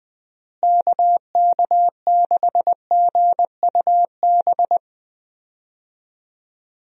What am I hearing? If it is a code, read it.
Morse code, heard as a keyed tone, about 20 wpm, 700 Hz. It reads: KK6GUB
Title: USA Call Signs - 6 Characters - Repeated 3x - 20wpm